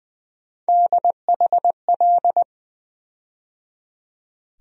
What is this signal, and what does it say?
Morse code, heard as a keyed tone, about 20 wpm, 700 Hz. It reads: DHL
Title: USA Call Sign Postfixes - 3 Letters 20wpm